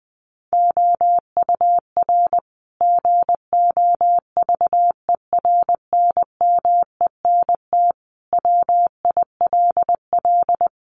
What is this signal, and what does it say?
Morse code, heard as a keyed tone, about 20 wpm, 700 Hz. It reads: OUR GOVERNMENT WILL
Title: Sentences from Top 1200 Words 20wpm